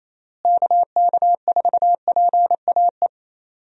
Morse code, heard as a keyed tone, about 28 wpm, 700 Hz. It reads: KK4PAE
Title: USA Call Signs - 6 Characters - Repeated 3x - 28wpm